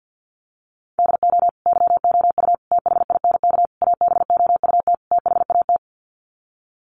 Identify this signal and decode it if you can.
Morse code, heard as a keyed tone, about 50 wpm, 700 Hz. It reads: DO YOU THINK ABOUT THAT
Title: Sets of 5 Words Encore2 50wpm